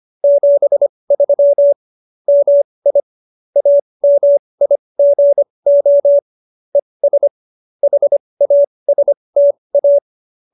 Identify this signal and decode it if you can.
Morse code, heard as a keyed tone, about 25 wpm, spaced as for 20 wpm, 570 Hz. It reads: 73 MI AMIGO ES HASTA